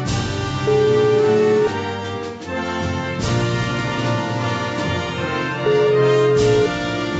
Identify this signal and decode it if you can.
Ringback tone (Europe, 425 Hz), with noise.